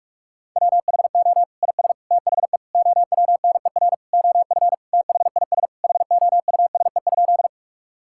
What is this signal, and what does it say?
Morse code, heard as a keyed tone, about 46 wpm, 700 Hz. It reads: WHO IS THE OWNER OF THIS HOUSE?
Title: Sentences from Top 1600 Words 45wpm